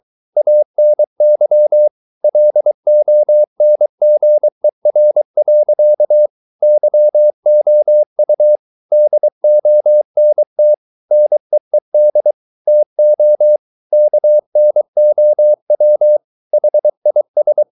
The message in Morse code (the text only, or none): ANY LONGER. YOU DONT NEED TO KNOW HIS